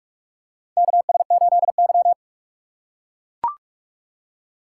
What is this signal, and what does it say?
Morse code, heard as a keyed tone, about 43 wpm, 700 Hz. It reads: KS8Y
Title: USA Call Signs - 4 Characters - 4x Character Spacing - 45wpm